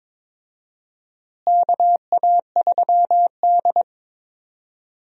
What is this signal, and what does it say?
Morse code, heard as a keyed tone, about 22 wpm, 700 Hz. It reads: KA3D
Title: USA Call Signs - 4 Characters - Repeated 2x - 22wpm